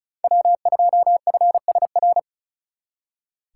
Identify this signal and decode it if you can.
Morse code, heard as a keyed tone, about 47 wpm, spaced as for 41 wpm, 700 Hz. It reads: W2FSR
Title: Speed Racing - USA Call Signs 35wpm